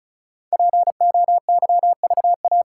Morse code, heard as a keyed tone, about 33 wpm, 700 Hz. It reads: POYVA
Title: Code Groups - 5 Letters - 35wpm